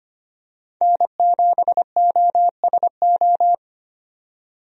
Morse code, heard as a keyed tone, about 25 wpm, 700 Hz. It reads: N7OSO
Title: USA Call Signs - 5 Characters - 25wpm